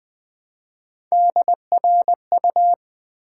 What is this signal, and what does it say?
Morse code, heard as a keyed tone, about 20 wpm, 700 Hz. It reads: DRU
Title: Build A Word - 1 Letter At A Time - 4-Letter Words - 20wpm